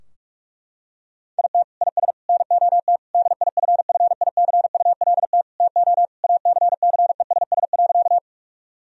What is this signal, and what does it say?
Morse code, heard as a keyed tone, about 43 wpm, 700 Hz. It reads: IT IS NOT DIFFICULT TO ACCESS.
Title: Sentences from Top 1000 Words - Review Part 4 - 45wpm